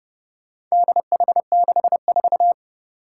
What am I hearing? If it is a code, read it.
Morse code, heard as a keyed tone, about 30 wpm, 700 Hz. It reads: DH64
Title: Four Challenging Letter-Number Combinations 30wpm